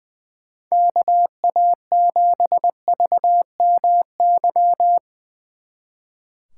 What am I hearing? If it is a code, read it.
Morse code, heard as a keyed tone, about 22 wpm, 700 Hz. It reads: KA7VMY
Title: Speed Racing - USA Call Signs 20wpm